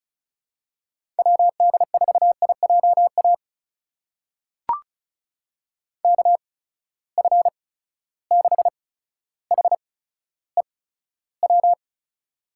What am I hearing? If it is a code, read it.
Morse code, heard as a keyed tone, about 36 wpm, spaced as for 15 wpm, 700 Hz. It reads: WD4IJA KF6HEW
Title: USA Call Signs - 6 Characters - 8x Character Spacing - 35wpm